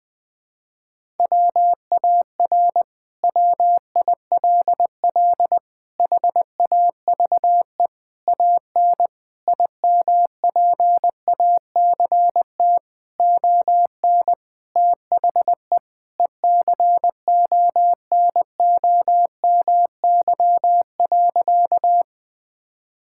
Morse code, heard as a keyed tone, about 20 wpm, 700 Hz. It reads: WAR WILL HAVE AN IMPACT ON THE ECONOMY.
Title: Sentences from Top 1000 Words - Review Part 6 - 20wpm